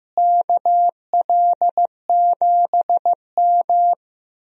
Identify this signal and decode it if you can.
Morse code, heard as a keyed tone, about 15 wpm, 700 Hz. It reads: KL7M
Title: DXCC-100 Call Signs - 4 Characters - Repeated 2x - 15wpm